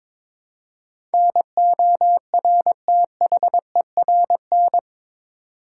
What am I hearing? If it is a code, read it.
Morse code, heard as a keyed tone, about 22 wpm, 700 Hz. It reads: NORTHERN
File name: practice / 22wpm / Eight Letter Words - 22wpm